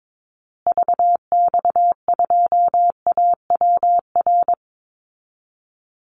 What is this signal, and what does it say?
Morse code, heard as a keyed tone, about 22 wpm, 700 Hz. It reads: VX2AWR
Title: Call Signs Repeated 2x - 22wpm